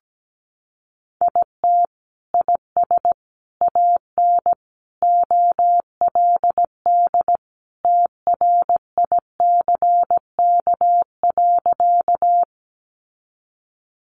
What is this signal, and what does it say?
Morse code, heard as a keyed tone, about 17 wpm, 700 Hz. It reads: IT IS AN OLD TRICK.
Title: Sentences from Top 1500 Words 17wpm